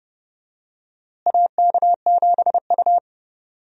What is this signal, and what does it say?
Morse code, heard as a keyed tone, about 32 wpm, 700 Hz. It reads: AK7U
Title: USA Call Signs - 4 Characters - Repeated 2x - 30wpm